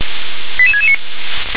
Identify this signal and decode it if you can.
Five-tone selective call (ZVEI), heard as quick runs of five tones, tones of about 70 ms. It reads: 88669